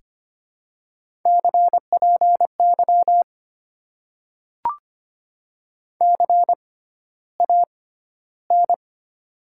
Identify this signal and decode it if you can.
Morse code, heard as a keyed tone, about 25 wpm, spaced as for 13 wpm, 700 Hz. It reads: CPY CAN
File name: Top 90 QSO Elements - 6x Character Spacing - 25wpm